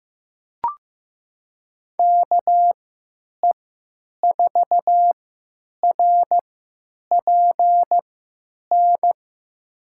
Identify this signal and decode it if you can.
Morse code, heard as a keyed tone, about 16 wpm, spaced as for 9 wpm, 700 Hz. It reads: KE4RPN
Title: USA Call Signs - 6 Characters - 3x Character Spacing - 15wpm